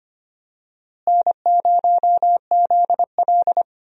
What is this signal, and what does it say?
Morse code, heard as a keyed tone, about 25 wpm, 700 Hz. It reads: N0ZL
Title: USA Call Signs - 4 Characters - Repeated 2x - 25wpm